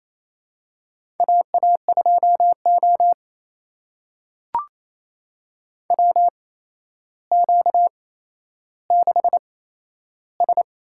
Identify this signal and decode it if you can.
Morse code, heard as a keyed tone, about 28 wpm, spaced as for 12 wpm, 700 Hz. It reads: AA2O WQ6S
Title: USA Call Signs - 4 Characters - 8x Character Spacing - 28wpm